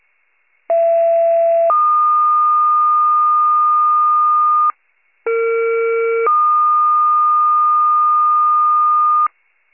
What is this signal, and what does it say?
Two-tone page, heard as a one-second tone, then a three-second tone, about 670 then 1220 Hz, about 460 then 1220 Hz. It is heard from 0.7 to 9.3 s.